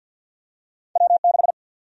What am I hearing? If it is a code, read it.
Morse code, heard as a keyed tone, about 50 wpm, 700 Hz. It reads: W6
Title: USA Call Sign Prefixes - 50wpm